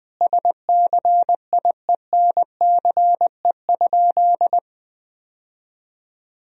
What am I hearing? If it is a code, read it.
Morse code, heard as a keyed tone, about 21 wpm, 700 Hz. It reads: SCIENCE?
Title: Sentences from Top 1000 Words - Review Part 2 - 20wpm